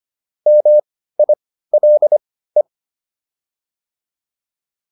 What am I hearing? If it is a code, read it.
Morse code, heard as a keyed tone, about 25 wpm, spaced as for 15 wpm, 600 Hz. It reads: MILE